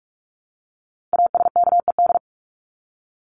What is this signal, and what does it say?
Morse code, heard as a keyed tone, about 44 wpm, 700 Hz. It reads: ASKED
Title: ED Suffix - Top 100 Words with Base Words 45wpm